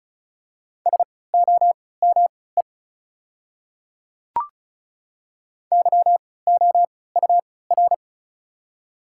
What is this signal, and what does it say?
Morse code, heard as a keyed tone, about 36 wpm, spaced as for 20 wpm, 690 Hz. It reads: SOME YOUR
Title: 3x Character Spacing - Top 300 Words - 35wpm